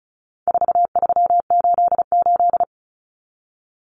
Morse code, heard as a keyed tone, about 34 wpm, 700 Hz. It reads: 4388
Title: Common Cut Numbers - 4-Digit 35wpm